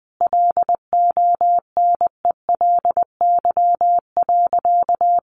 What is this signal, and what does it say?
Morse code, heard as a keyed tone, about 20 wpm, 700 Hz. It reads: LONELY.
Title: Additional Easy Phrases 20wpm